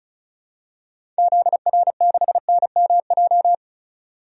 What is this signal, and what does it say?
Morse code, heard as a keyed tone, about 34 wpm, 700 Hz. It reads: ZR6NMJ
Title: DXCC-100 Call Signs - 6 Characters - 35wpm